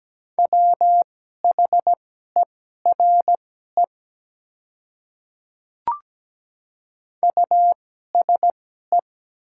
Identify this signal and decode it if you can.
Morse code, heard as a keyed tone, about 17 wpm, spaced as for 12 wpm, 700 Hz. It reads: WHERE USE
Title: 2x Character Spacing - Top 200 Words - 17wpm